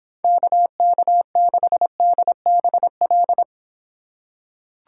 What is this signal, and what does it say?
Morse code, heard as a keyed tone, about 25 wpm, 700 Hz. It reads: KK6DBL